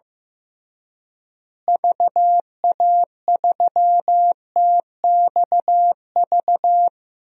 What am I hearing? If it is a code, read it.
Morse code, heard as a keyed tone, about 15 wpm, 700 Hz. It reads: VA3TXV
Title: DXCC-100 Call Signs - 6 Characters - Repeated 2x - 15wpm